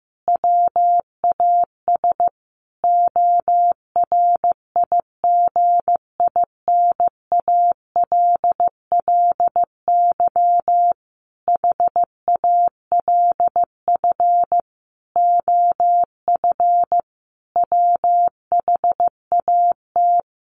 Morse code, heard as a keyed tone, about 15 wpm, 700 Hz. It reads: WAS ORIGINALLY HALF OF WHAT
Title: Sentences from Top 1500 Words 15wpm